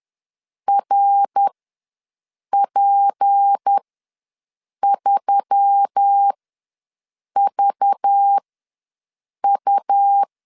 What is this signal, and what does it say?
Morse code, heard as a keyed tone, about 10 wpm, spaced as for 6 wpm, 800 Hz. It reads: RP3VU
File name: LEZIONI / 7 GF9Z / 7 GF9 ws 10wpm 4034